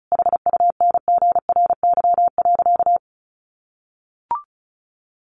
Morse code, heard as a keyed tone, about 34 wpm, 700 Hz. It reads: HUNGRY.
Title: Sentences from Top 1400 Words 35wpm